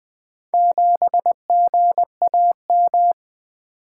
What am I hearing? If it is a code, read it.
Morse code, heard as a keyed tone, about 20 wpm, 700 Hz. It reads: 7GAM
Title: Four Letter-Number Combinations - 20wpm